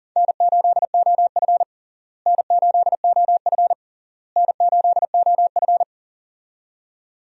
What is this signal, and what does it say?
Morse code, heard as a keyed tone, about 40 wpm, 700 Hz. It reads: N8OF N8OF N8OF
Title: USA Call Signs - 4 Characters - Repeated 3x - 40wpm